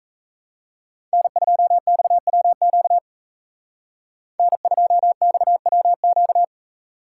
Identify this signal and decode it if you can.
Morse code, heard as a keyed tone, about 38 wpm, 700 Hz. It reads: N2XWQ N2XWQ